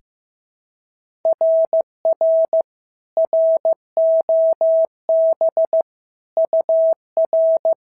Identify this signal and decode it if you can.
Morse code, heard as a keyed tone, about 14 wpm, 650 Hz. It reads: RR ROB UR
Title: SOTA QSO Practice 15wpm